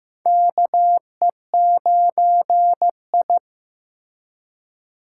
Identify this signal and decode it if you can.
Morse code, heard as a keyed tone, about 15 wpm, 700 Hz. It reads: KE9I
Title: K1USN SST QSO Practice - 15wpm